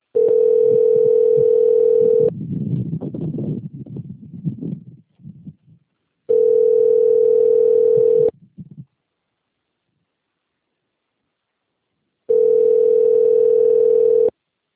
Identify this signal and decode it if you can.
Ringback tone (North America), 440 + 480 Hz.